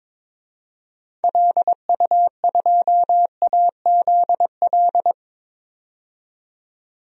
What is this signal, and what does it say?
Morse code, heard as a keyed tone, about 21 wpm, 700 Hz. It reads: LU2AZL
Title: DXCC-100 Call Signs - 6 Characters - 22wpm